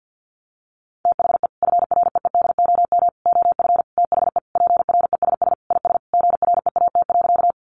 Morse code, heard as a keyed tone, about 50 wpm, 700 Hz. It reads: THE FREEDOM OF THE PRESS IS GREAT.
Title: Sentences from Top 800 Words 50wpm